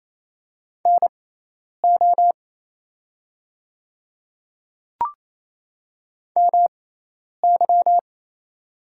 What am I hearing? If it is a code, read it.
Morse code, heard as a keyed tone, about 28 wpm, spaced as for 10 wpm, 700 Hz. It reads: NO MY
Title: 6x Character Spacing - Top 200 Words - 28wpm